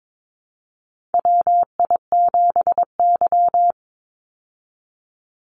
Morse code, heard as a keyed tone, about 22 wpm, 700 Hz. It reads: WI7Y
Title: USA Call Signs 22wpm